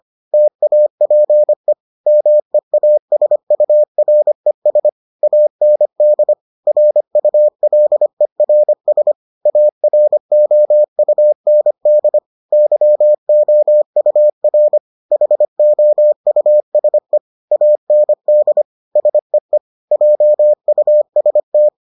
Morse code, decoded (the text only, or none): TAPE MEASURES AND RULERS AROUND YOUR HOUSE AND SEE JUST